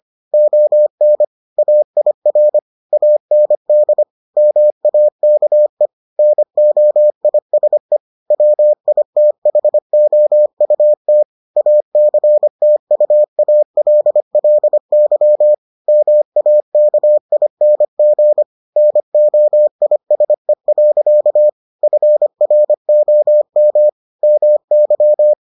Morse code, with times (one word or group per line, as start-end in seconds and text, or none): 0.3-1.2 ON
1.6-2.6 AIR
2.9-4.0 AND
4.4-5.9 MAKE
6.2-8.0 NOISE
8.3-11.2 WITHOUT
11.6-15.5 ACTUALLY
15.9-18.4 MAKING
18.8-21.5 NOISE.
21.8-23.9 FROM
24.2-25.3 MY